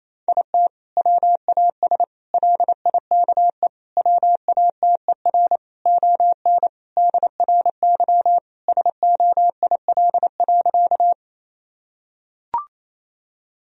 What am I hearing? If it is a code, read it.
Morse code, heard as a keyed tone, about 28 wpm, 700 Hz. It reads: IT WAS LIKE WATER ON DRY SOIL.
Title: Sentences from Top 1000 Words - Review Part 4 - 28wpm